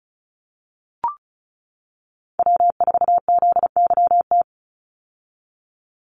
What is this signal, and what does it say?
Morse code, heard as a keyed tone, about 34 wpm, 700 Hz. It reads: W4ZYT
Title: Call Signs - 35wpm